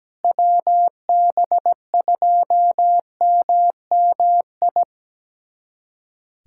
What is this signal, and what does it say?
Morse code, heard as a keyed tone, about 17 wpm, 700 Hz. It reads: WB2MMI
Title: USA Callsign DE Callsign - 6 Characters - 17wpm